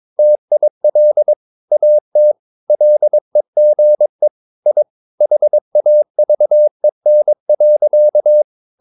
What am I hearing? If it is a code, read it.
Morse code, heard as a keyed tone, about 22 wpm, 600 Hz. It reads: TIL AT LEGE I HAVEN.
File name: Ferie QRS 22wpm